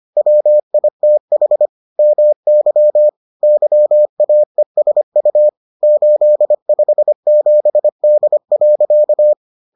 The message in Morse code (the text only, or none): WITH MY YAESU 857D.